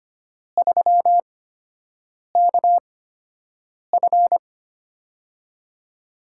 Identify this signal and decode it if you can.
Morse code, heard as a keyed tone, about 25 wpm, spaced as for 7 wpm, 700 Hz. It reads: 3KF
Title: USA Call Signs - 4 Characters - 8x Character Spacing - 25wpm